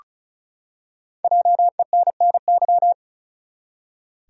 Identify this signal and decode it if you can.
Morse code, heard as a keyed tone, about 35 wpm, 700 Hz. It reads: JENNY
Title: USA Common Names - Max 5 Letters - 35wpm